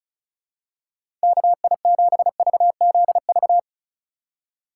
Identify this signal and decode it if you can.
Morse code, heard as a keyed tone, about 36 wpm, 700 Hz. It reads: KI7VZV
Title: USA Call Signs - 6 Characters - 3x Character Spacing - 35wpm